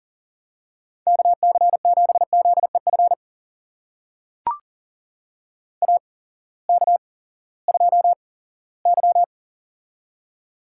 Morse code, heard as a keyed tone, about 38 wpm, spaced as for 20 wpm, 700 Hz. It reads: KC7ZEF AK2Y
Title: USA Call Signs - 8x Character Spacing - 40wpm